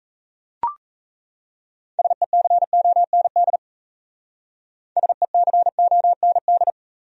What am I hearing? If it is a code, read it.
Morse code, heard as a keyed tone, about 32 wpm, 700 Hz. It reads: SECOND SECOND
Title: Speed-Racing Top 500 Words 28wpm